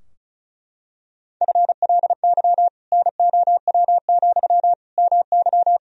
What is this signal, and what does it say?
Morse code, heard as a keyed tone, about 36 wpm, 700 Hz. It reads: FLY NOW, MY
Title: Sentences from Top 1000 Words - Review Part 2 - 35wpm